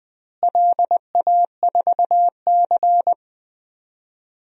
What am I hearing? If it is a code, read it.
Morse code, heard as a keyed tone, about 20 wpm, 700 Hz. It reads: LA4C
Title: DXCC-100 Call Signs - 4 Characters - Repeated 2x - 20wpm